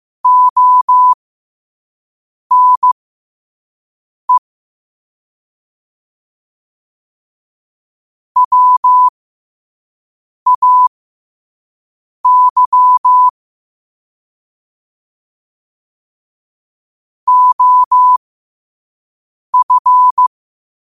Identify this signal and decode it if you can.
Morse code, heard as a keyed tone, about 15 wpm, spaced as for 5 wpm, 1000 Hz. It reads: ONE WAY OF
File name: qotdfiles / 5wpm